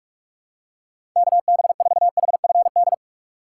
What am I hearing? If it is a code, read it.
Morse code, heard as a keyed tone, about 46 wpm, 700 Hz. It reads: KB4HFD
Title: USA Call Signs - 6 Characters - 6x Character Spacing - 45wpm